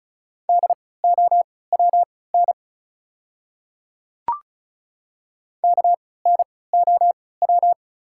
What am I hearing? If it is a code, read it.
Morse code, heard as a keyed tone, about 36 wpm, spaced as for 20 wpm, 700 Hz. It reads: DOWN KNOW
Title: 3x Character Spacing - Top 100 Words - 35wpm